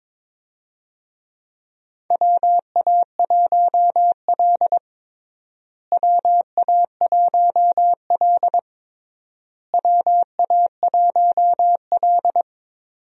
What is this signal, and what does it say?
Morse code, heard as a keyed tone, about 22 wpm, 700 Hz. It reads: WA1L WA1L WA1L
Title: USA Call Signs - 4 Characters - Repeated 3x - 22wpm